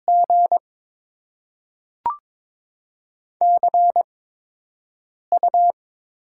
Morse code, heard as a keyed tone, about 21 wpm, spaced as for 6 wpm, 700 Hz. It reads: G CU
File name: Top 20 QSO Elements - 8x Character Spacing - 22wpm